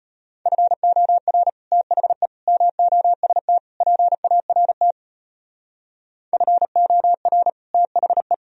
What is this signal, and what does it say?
Morse code, heard as a keyed tone, about 31 wpm, 700 Hz. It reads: FOR THE MOST PART FOR THE
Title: Speed-Racing - Sets of 4 Words 25wpm